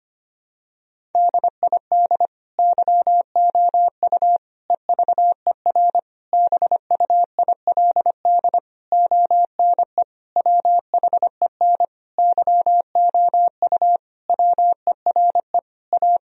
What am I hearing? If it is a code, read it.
Morse code, heard as a keyed tone, about 25 wpm, 700 Hz. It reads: DID YOU EVER BUILD ONE WHEN YOU WERE A
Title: Sentences from Top 1000 Words - Review Part 1 - 25wpm